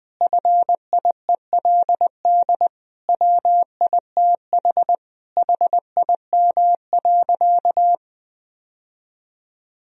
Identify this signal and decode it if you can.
Morse code, heard as a keyed tone, about 19 wpm, 700 Hz. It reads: FIELD WITH HIM.
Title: Sentences from Top 1000 Words - Review Part 2 - 20wpm